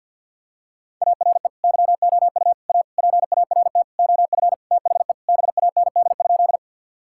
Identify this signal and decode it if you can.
Morse code, heard as a keyed tone, about 50 wpm, 700 Hz. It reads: ARE YOU A PART OF THE BAND?
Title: Sentences from Top 1600 Words 50wpm